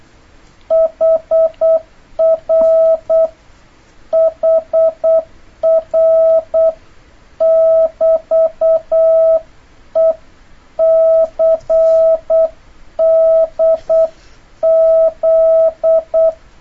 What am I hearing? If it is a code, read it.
Morse code, heard as a keyed tone, about 8 wpm, 650 Hz. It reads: HR HR <BT>E CDZ
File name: jh8cbh / 2026cw